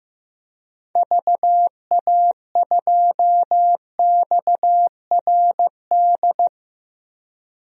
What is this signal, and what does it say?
Morse code, heard as a keyed tone, about 15 wpm, 700 Hz. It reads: VA2XRD
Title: DXCC-100 Call Signs - 6 Characters - Repeated 3x - 15wpm